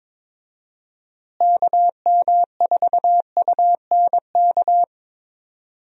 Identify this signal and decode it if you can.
Morse code, heard as a keyed tone, about 22 wpm, 700 Hz. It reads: KM4UNK
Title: USA Call Signs - 6 Characters - Repeated 2x - 22wpm